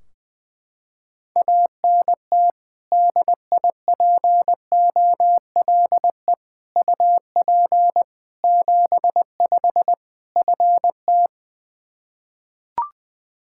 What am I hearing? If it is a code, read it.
Morse code, heard as a keyed tone, about 20 wpm, 700 Hz. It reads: ANT DIPOLE UP 75 FT
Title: Practice QSO - Exchange 2 - Info - 20wpm